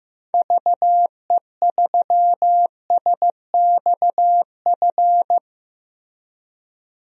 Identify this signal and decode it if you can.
Morse code, heard as a keyed tone, about 16 wpm, 700 Hz. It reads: VE3SXF
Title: DXCC-100 Call Signs - 6 Characters - Repeated 2x - 15wpm